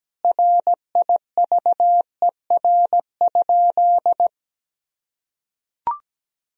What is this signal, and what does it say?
Morse code, heard as a keyed tone, about 17 wpm, 700 Hz. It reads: RIVER?
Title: Sentences from Top 1000 Words - Review Part 4 - 17wpm